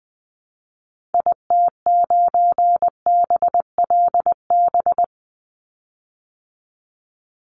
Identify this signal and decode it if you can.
Morse code, heard as a keyed tone, about 20 wpm, 700 Hz. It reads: IT9BLB
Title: Call Signs Repeated 3x - 20wpm